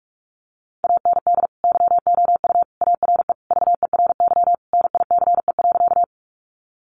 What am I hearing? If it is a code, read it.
Morse code, heard as a keyed tone, about 44 wpm, 700 Hz. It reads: AND YOU ARE VERY NICE.
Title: Sentences from Top 1100 Words 45wpm